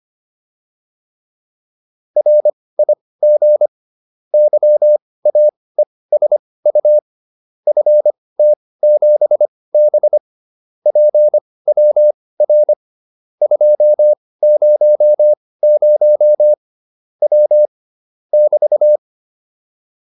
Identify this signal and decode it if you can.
Morse code, heard as a keyed tone, about 25 wpm, spaced as for 18 wpm, 600 Hz. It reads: RIG YAESU FT7B PWR 200 W <BT>